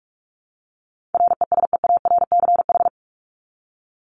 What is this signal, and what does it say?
Morse code, heard as a keyed tone, about 44 wpm, 700 Hz. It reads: RESEARCH
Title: Top 501-750 Words 45wpm